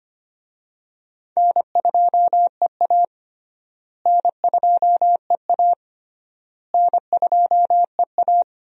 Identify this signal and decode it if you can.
Morse code, heard as a keyed tone, about 25 wpm, 700 Hz. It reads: N2EA N2EA N2EA
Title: USA Call Signs - 4 Characters - Repeated 3x - 25wpm